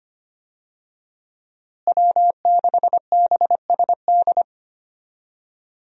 Morse code, heard as a keyed tone, about 25 wpm, 700 Hz. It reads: W6BSD
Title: USA Call Signs - 5 Characters - Repeated 3x - 25wpm